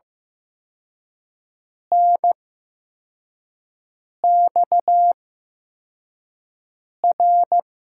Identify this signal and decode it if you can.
Morse code, heard as a keyed tone, about 15 wpm, spaced as for 4 wpm, 700 Hz. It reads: NXR
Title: USA Call Signs - 5 Characters - 8x Character Spacing - 15wpm